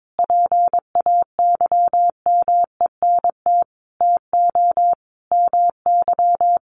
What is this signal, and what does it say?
Morse code, heard as a keyed tone, about 22 wpm, 700 Hz. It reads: PAYMENT TO MY
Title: Sentences from Top 1400 Words 22wpm